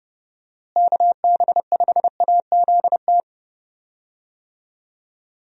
Morse code, heard as a keyed tone, about 31 wpm, 700 Hz. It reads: KB5AZT
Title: USA Call Signs - 6 Characters - Repeated 3x - 30wpm